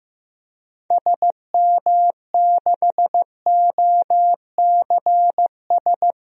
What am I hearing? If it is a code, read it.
Morse code, heard as a keyed tone, about 15 wpm, 700 Hz. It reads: SM6OCS
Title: DXCC-100 Call Signs - 6 Characters - 15wpm